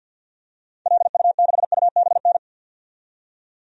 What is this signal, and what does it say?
Morse code, heard as a keyed tone, about 46 wpm, 700 Hz. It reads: LU6UBN
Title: DXCC-100 Call Signs - 6 Characters - 50wpm